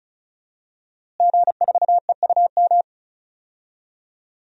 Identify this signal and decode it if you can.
Morse code, heard as a keyed tone, about 36 wpm, 700 Hz. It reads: G4EUM
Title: DXCC-100 Call Signs - 5 Characters - Repeated 3x - 35wpm